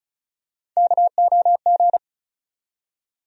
Morse code, heard as a keyed tone, about 34 wpm, 700 Hz. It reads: KOG
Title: USA Call Sign Postfixes - 3 Letters 35wpm